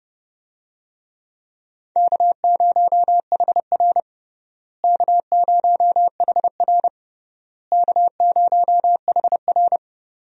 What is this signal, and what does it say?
Morse code, heard as a keyed tone, about 30 wpm, 700 Hz. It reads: K0HR K0HR K0HR
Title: USA Call Signs - 4 Characters - Repeated 3x - 30wpm